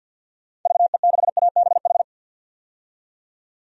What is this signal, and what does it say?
Morse code, heard as a keyed tone, about 46 wpm, 700 Hz. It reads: VE6ABH
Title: DXCC-100 Call Signs - 6 Characters - 50wpm